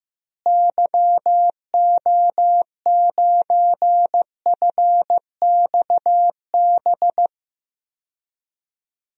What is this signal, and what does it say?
Morse code, heard as a keyed tone, about 15 wpm, 700 Hz. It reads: YO9FXB
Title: DXCC-100 Call Signs - 6 Characters - Repeated 3x - 15wpm